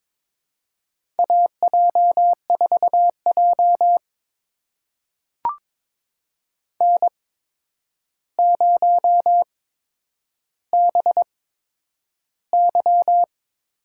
Morse code, heard as a keyed tone, about 22 wpm, spaced as for 9 wpm, 700 Hz. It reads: AJ4J N0BY
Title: USA Call Signs - 4 Characters - 8x Character Spacing - 22wpm